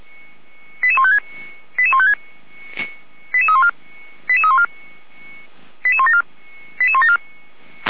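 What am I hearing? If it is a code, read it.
Five-tone selective call (ZVEI), heard as quick runs of five tones, tones of about 80 ms. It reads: 88156 88156 88314 88314 88174 88174
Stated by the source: ZVEI five-tone call received by SDR.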